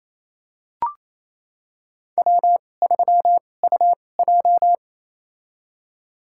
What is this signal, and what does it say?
Morse code, heard as a keyed tone, about 28 wpm, spaced as for 20 wpm, 700 Hz. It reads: W3UJ
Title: USA Call Signs - 4 Characters - 2x Character Spacing - 28wpm